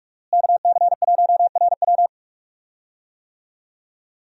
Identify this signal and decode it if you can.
Morse code, heard as a keyed tone, about 42 wpm, 700 Hz. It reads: KC1RW
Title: USA Call Signs - 5 Characters - Repeated 2x - 45wpm